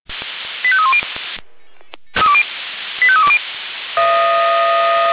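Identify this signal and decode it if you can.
Five-tone selective call (ZVEI), heard as quick runs of five tones, tones of about 70 ms. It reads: 85310 85320
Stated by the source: ZVEI five-tone call received by SDR, archived by fire brigade Feuerwehr Orlamünde.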